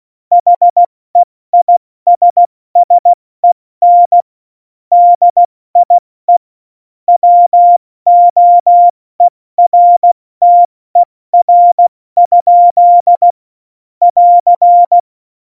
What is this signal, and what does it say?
Morse code, heard as a keyed tone, about 16 wpm, 700 Hz. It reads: HEISSEN DIE WOERTER? <AR>
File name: Morsekurs / Witze / Witz 08. August 17, Tempo 70 F80